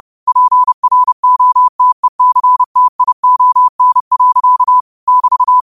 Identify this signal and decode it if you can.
Morse code, heard as a keyed tone, about 30 wpm, 1000 Hz. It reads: PROTECTION. <BT>
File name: qotdfiles / 30wpm